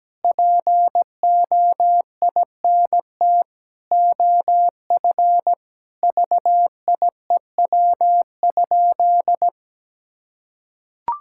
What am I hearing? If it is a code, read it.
Morse code, heard as a keyed tone, about 17 wpm, 700 Hz. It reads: POINT OF VIEW?
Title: Sentences from Top 1000 Words - Review Part 6 - 17wpm